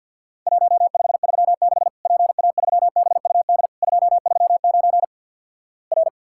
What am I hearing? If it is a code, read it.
Morse code, heard as a keyed tone, about 50 wpm, 700 Hz. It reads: 1536 PA3BUD 239 R
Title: Worked All Europe DX Contest - QTC Exchange - 50wpm